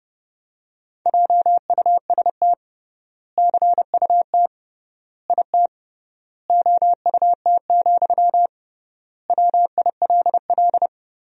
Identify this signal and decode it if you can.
Morse code, heard as a keyed tone, about 30 wpm, 700 Hz. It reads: JUST CUT IT OUT, WILL
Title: 3x Word-Spacing - Sentences from Top 300 Words - 30wpm